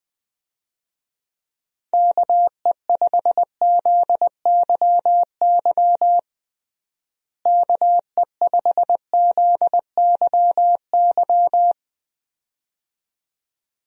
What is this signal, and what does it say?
Morse code, heard as a keyed tone, about 21 wpm, 700 Hz. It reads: KE5ZYY KE5ZYY
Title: USA Call Signs - 6 Characters - Repeated 2x - 20wpm